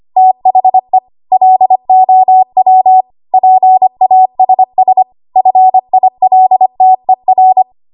Morse code, heard as a keyed tone, about 25 wpm, 750 Hz. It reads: THE LOW PASS FILTER
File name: Archive / 25 WPM / 25WPM